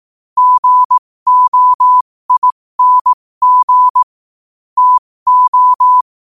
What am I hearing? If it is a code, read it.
Morse code, heard as a keyed tone, about 18 wpm, spaced as for 15 wpm, 1000 Hz. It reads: GOING TO
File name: qotdfiles / 15wpm